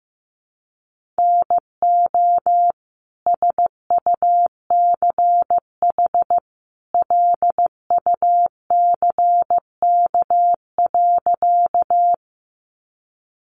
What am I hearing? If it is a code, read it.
Morse code, heard as a keyed tone, about 15 wpm, 700 Hz. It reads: NO SUCH LUCK.
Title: Sentences from Top 1500 Words 15wpm